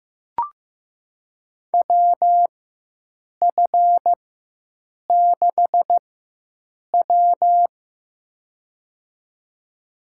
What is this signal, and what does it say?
Morse code, heard as a keyed tone, about 15 wpm, spaced as for 7 wpm, 700 Hz. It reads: WF6W
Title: USA Call Signs - 4x Character Spacing - 15wpm